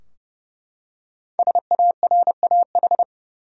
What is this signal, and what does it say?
Morse code, heard as a keyed tone, about 32 wpm, 700 Hz. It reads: SARAH
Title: USA Common Names - 30wpm